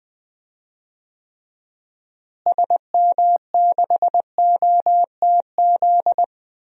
Morse code, heard as a keyed tone, about 20 wpm, 700 Hz. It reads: SM6OTZ
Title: DXCC-100 Call Signs - 6 Characters - Repeated 2x - 20wpm